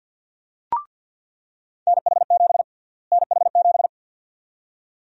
Morse code, heard as a keyed tone, about 46 wpm, 700 Hz. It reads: NH7 NH7
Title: USA Call Sign Prefixes - Repeated 2x - 50wpm